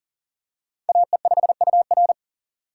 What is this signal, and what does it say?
Morse code, heard as a keyed tone, about 40 wpm, 700 Hz. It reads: AE5UR
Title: USA Call Signs - 5 Characters - 2x Character Spacing - 40wpm